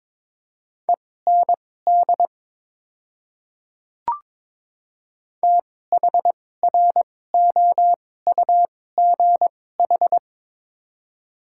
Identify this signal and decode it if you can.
Morse code, heard as a keyed tone, about 22 wpm, spaced as for 16 wpm, 700 Hz. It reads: END THROUGH
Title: 2x Character Spacing - Top 200 Words - 22wpm